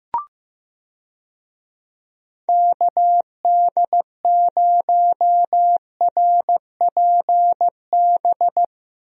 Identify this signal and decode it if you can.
Morse code, heard as a keyed tone, about 15 wpm, 700 Hz. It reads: KD0RPB
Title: USA Call Signs - 6 Characters - Repeated 3x - 15wpm